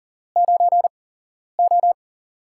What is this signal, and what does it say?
Morse code, heard as a keyed tone, about 40 wpm, spaced as for 11 wpm, 700 Hz. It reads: 9O
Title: USA Call Signs - 8x Character Spacing - 40wpm